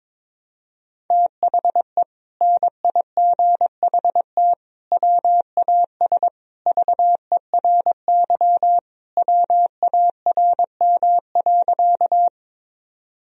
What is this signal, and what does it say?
Morse code, heard as a keyed tone, about 22 wpm, 700 Hz. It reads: THE NIGHT WAS VERY WARM.
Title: Sentences from Top 1000 Words - Review Part 2 - 22wpm